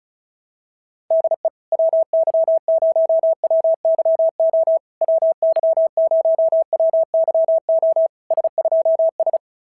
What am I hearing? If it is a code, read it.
Morse code, heard as a keyed tone, about 34 wpm, 650 Hz. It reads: DE WY0WYO WY0WYO S2S
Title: SOTA QSO Practice 35wpm